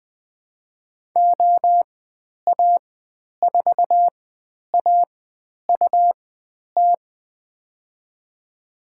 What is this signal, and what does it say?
Morse code, heard as a keyed tone, about 21 wpm, spaced as for 10 wpm, 700 Hz. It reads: OA4AUT
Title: 109 Lesson 18g - Letter m - Call Signs 20wpm